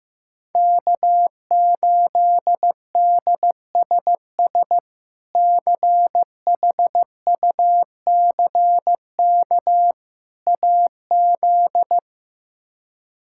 Morse code, heard as a keyed tone, about 15 wpm, 700 Hz. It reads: K8DSS CHUCK AZ
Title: K1USN SST QSO Practice - 15wpm